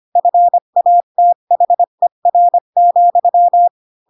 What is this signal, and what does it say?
Morse code, heard as a keyed tone, about 25 wpm, 700 Hz. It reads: FATHER,
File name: Pinocchio / Pinocchio ch7 20 25 ik1hge